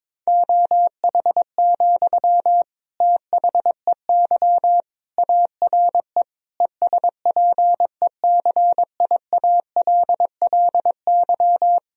Morse code, heard as a keyed tone, about 22 wpm, 700 Hz. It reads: OH, THEY ARE ESPECIALLY
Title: Sentences from Top 1000 Words - Review Part 4 - 22wpm